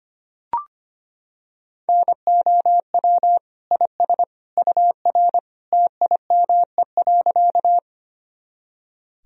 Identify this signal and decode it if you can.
Morse code, heard as a keyed tone, about 25 wpm, 700 Hz. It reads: NOW IS UR TIME.
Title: Sentences from Top 100 Words and Abbreviations 25wpm